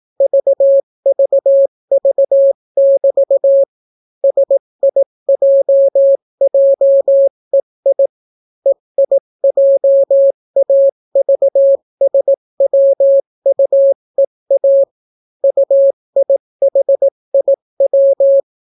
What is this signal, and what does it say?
Morse code, heard as a keyed tone, about 18 wpm, 550 Hz. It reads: VVV<BT> SIJJEI EIJAVSWUEA UIHIW